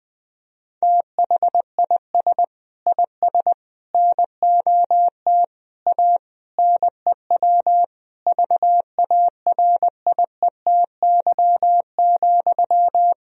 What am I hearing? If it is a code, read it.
Morse code, heard as a keyed tone, about 20 wpm, 700 Hz. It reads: THIS IS NOT A NEW VARIETY,
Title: Sentences from Top 1000 Words - Review Part 4 - 20wpm